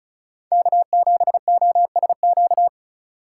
Morse code, heard as a keyed tone, about 36 wpm, 700 Hz. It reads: K7OSQ
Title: USA Call Signs - 5 Characters - 4x Character Spacing - 35wpm